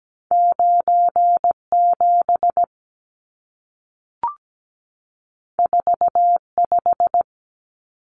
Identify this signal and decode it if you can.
Morse code, heard as a keyed tone, about 17 wpm, 700 Hz. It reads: 97 45
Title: Numbers - 2-Digit 17wpm